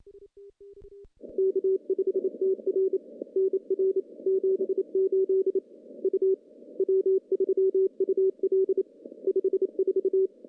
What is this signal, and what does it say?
Morse code, heard as a keyed tone, about 29 wpm, 380 Hz. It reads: K5TR NR 78 U W3UL 54
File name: contests / CW-SS-k5tr- -0000